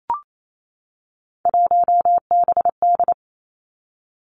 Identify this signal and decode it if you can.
Morse code, heard as a keyed tone, about 28 wpm, 700 Hz. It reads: 1BD
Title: Three Challenging Letter-Number Combinations 28wpm